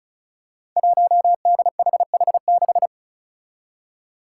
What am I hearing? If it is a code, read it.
Morse code, heard as a keyed tone, about 36 wpm, 700 Hz. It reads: 1DHH6
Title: Five Challenging Letter-Number Combinations 35wpm